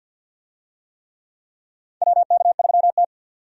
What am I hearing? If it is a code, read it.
Morse code, heard as a keyed tone, about 51 wpm, 700 Hz. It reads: WK3T
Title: USA Call Signs - 4 Characters - 50wpm